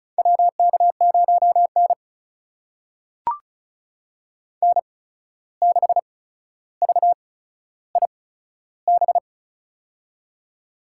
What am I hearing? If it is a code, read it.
Morse code, heard as a keyed tone, about 36 wpm, spaced as for 14 wpm, 700 Hz. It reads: WK0N N6VIB
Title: USA Call Signs - 8x Character Spacing - 35wpm